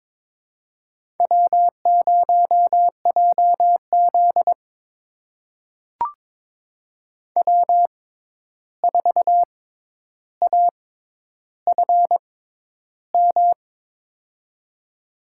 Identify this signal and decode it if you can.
Morse code, heard as a keyed tone, about 21 wpm, spaced as for 10 wpm, 700 Hz. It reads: W0JZ W4AFM
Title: USA Call Signs - 6x Character Spacing - 22wpm